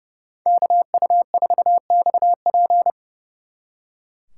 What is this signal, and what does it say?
Morse code, heard as a keyed tone, about 36 wpm, 700 Hz. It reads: KU4XP
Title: Speed Racing - USA Call Signs 30wpm